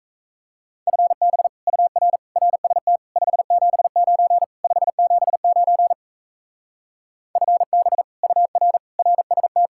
Morse code, heard as a keyed tone, about 35 wpm, 700 Hz. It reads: FB UR RST 579 579 FB UR RST